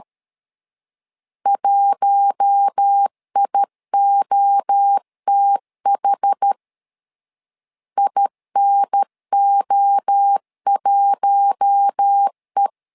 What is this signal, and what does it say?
Morse code, heard as a keyed tone, about 13 wpm, 800 Hz. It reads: 1IOTH INO1E
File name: LEZIONI / 4 Q1 12wpm 2028